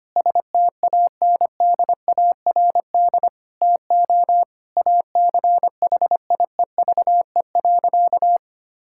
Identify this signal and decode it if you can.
Morse code, heard as a keyed tone, about 25 wpm, 700 Hz. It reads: STANDARD TO ACHIEVE.
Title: Sentences from Top 1000 Words - Review Part 6 - 25wpm